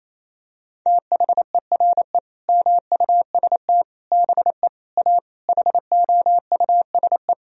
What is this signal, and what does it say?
Morse code, heard as a keyed tone, about 28 wpm, 700 Hz. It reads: THERE MUST BE A HOUSE
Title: Sentences from Top 1000 Words - Review Part 1 - 28wpm